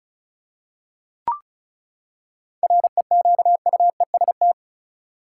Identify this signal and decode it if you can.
Morse code, heard as a keyed tone, about 36 wpm, 700 Hz. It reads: REQUEST
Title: Seven Letter Words - 35wpm